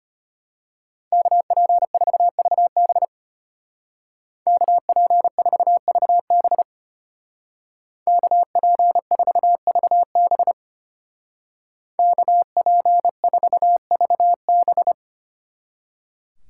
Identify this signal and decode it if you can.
Morse code, heard as a keyed tone, about 33 wpm, 700 Hz. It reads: KP4VB KP4VB KP4VB KP4VB